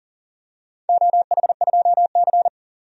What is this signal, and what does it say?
Morse code, heard as a keyed tone, about 37 wpm, 700 Hz. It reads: OH2C
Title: DXCC-100 Call Signs - 4 Characters - 40wpm